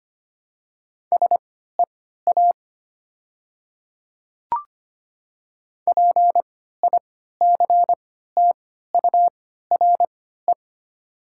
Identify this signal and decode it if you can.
Morse code, heard as a keyed tone, about 25 wpm, spaced as for 14 wpm, 700 Hz. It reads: SEA PICTURE
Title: 3x Character Spacing - Top 500 Words - 25wpm